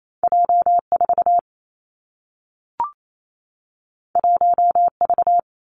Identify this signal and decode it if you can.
Morse code, heard as a keyed tone, about 28 wpm, 700 Hz. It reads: J 4 1 V
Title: Two Challenging Letter-Number Combinations 28wpm